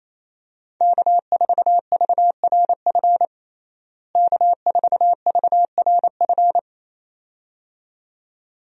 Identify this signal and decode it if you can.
Morse code, heard as a keyed tone, about 28 wpm, 700 Hz. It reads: K4VRF K4VRF
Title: USA Call Signs - 5 Characters - Repeated 2x - 28wpm